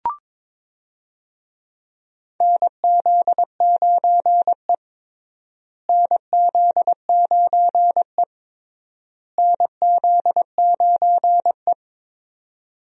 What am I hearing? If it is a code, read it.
Morse code, heard as a keyed tone, about 22 wpm, 700 Hz. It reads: NZ9E NZ9E NZ9E
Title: USA Call Signs - 4 Characters - Repeated 3x - 22wpm